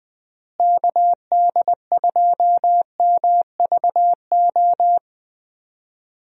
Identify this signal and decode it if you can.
Morse code, heard as a keyed tone, about 20 wpm, 700 Hz. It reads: KD2MVO
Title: USA Call Signs - 6 Characters - Repeated 2x - 20wpm